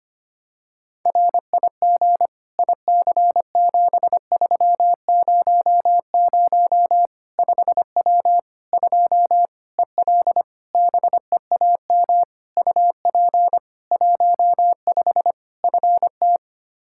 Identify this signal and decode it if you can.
Morse code, heard as a keyed tone, about 25 wpm, 700 Hz. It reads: RIG IC7300 5W 2 EL BEAM UP 15 FT
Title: K1USN SST QSO Practice 25wpm